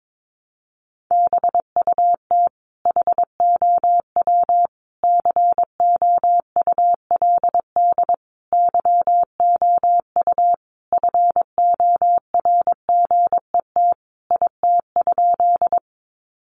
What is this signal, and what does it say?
Morse code, heard as a keyed tone, about 22 wpm, 700 Hz. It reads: BUT HOW COULD YOU FORGET IT?
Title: Sentences from Top 1200 Words 22wpm